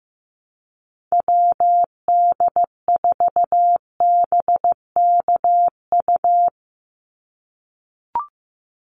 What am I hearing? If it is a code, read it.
Morse code, heard as a keyed tone, about 15 wpm, 700 Hz. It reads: WD4BKU
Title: USA Call Signs 15wpm